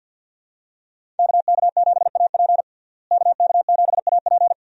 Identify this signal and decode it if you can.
Morse code, heard as a keyed tone, about 52 wpm, 700 Hz. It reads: KK7AP KK7AP
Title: USA Call Signs - 5 Characters - Repeated 3x - 50wpm